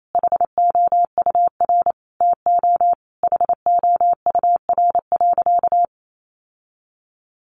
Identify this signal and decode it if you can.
Morse code, heard as a keyed tone, about 28 wpm, 700 Hz. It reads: HOUR TO HOUR.
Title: Sentences from Top 1200 Words 28wpm